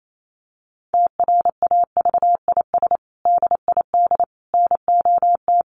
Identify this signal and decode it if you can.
Morse code, heard as a keyed tone, about 28 wpm, 700 Hz. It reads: TRAVIS DID NOT
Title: A Sound Of Thunder 28wpm